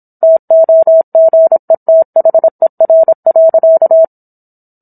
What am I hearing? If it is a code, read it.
Morse code, heard as a keyed tone, about 27 wpm, 650 Hz. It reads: TOGETHER.